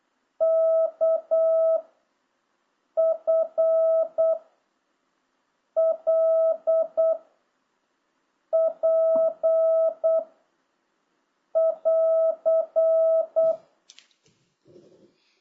Morse code, heard as a keyed tone, about 8 wpm, 660 Hz. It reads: K F L P <AR>